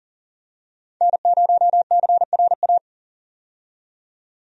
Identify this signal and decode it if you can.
Morse code, heard as a keyed tone, about 40 wpm, 700 Hz. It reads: N0CRA
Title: USA Call Signs - 5 Characters - 40wpm